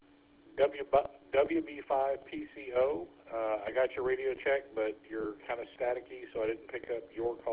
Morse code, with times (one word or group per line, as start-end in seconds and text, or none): none